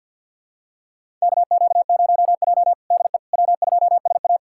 Morse code, heard as a keyed tone, about 46 wpm, 700 Hz. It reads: KQ0J DE W2SA
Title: USA Callsign DE Callsign - 4 Characters - 50wpm